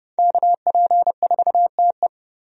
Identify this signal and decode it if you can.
Morse code, heard as a keyed tone, about 30 wpm, 700 Hz. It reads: KP4TE
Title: DXCC-100 Call Signs - 5 Characters - Repeated 2x - 30wpm